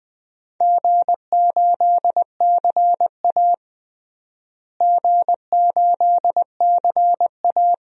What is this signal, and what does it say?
Morse code, heard as a keyed tone, about 21 wpm, 700 Hz. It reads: G8CA G8CA
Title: DXCC-100 Call Signs - 4 Characters - Repeated 2x - 20wpm